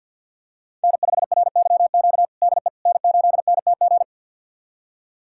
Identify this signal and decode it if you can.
Morse code, heard as a keyed tone, about 51 wpm, 700 Hz. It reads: N5RYQ DE N8NTG
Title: USA Callsign DE Callsign - 5 Characters - 50wpm